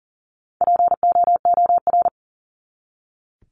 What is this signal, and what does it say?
Morse code, heard as a keyed tone, about 40 wpm, 700 Hz. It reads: POOR
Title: Top 751-1000 Words 40wpm